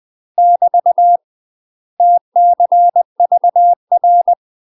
Morse code, heard as a keyed tone, about 20 wpm, 700 Hz. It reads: <BT> TCVR
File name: UK MASTER QSOs TRAINING / 20WPM / UK MASTER QSOs 0028